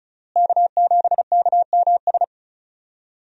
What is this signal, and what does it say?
Morse code, heard as a keyed tone, about 36 wpm, 700 Hz. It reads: K7KMS
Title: USA Call Signs - 5 Characters - Repeated 3x - 35wpm